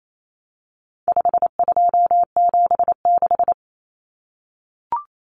Morse code, heard as a keyed tone, about 28 wpm, 700 Hz. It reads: 5276
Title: Numbers - 4-Digit 28wpm